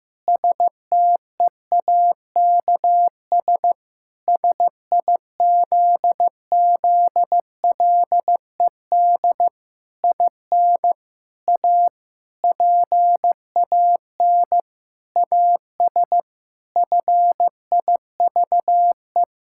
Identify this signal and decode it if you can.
Morse code, heard as a keyed tone, about 15 wpm, 700 Hz. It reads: STEAKS SIZZLED IN A PAN AS FIVE
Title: Pangram Challenge v2 - 15wpm